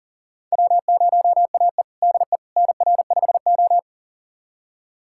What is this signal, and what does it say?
Morse code, heard as a keyed tone, about 40 wpm, 700 Hz. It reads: W0AE DE NR5O
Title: USA Callsign DE Callsign - 4 Characters - 40wpm